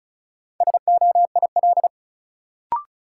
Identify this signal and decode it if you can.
Morse code, heard as a keyed tone, about 36 wpm, 700 Hz. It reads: SOIL
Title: Sentences from Top 1000 Words - Review Part 4 - New Word Drill - 35wpm